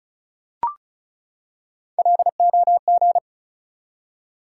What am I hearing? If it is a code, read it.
Morse code, heard as a keyed tone, about 36 wpm, 700 Hz. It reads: LOG
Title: Three Letter Words - 35wpm